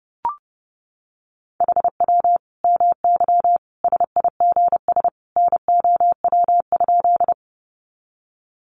Morse code, heard as a keyed tone, about 30 wpm, 700 Hz. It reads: HW MY SIGS NOW?
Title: Common QSO Phrases 30wpm